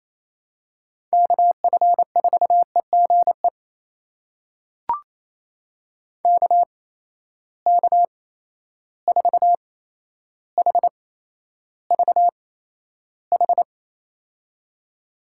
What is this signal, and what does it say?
Morse code, heard as a keyed tone, about 28 wpm, spaced as for 12 wpm, 700 Hz. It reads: KF4EGE KK4HVH
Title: USA Call Signs - 6 Characters - 8x Character Spacing - 28wpm